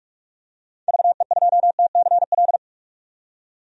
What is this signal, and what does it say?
Morse code, heard as a keyed tone, about 46 wpm, 700 Hz. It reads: VE2TCL
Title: DXCC-100 Call Signs - 6 Characters - 45wpm